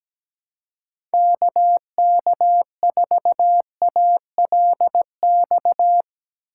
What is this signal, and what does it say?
Morse code, heard as a keyed tone, about 17 wpm, 700 Hz. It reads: KK4ALX
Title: USA Call Signs - 6 Characters - 3x Character Spacing - 17wpm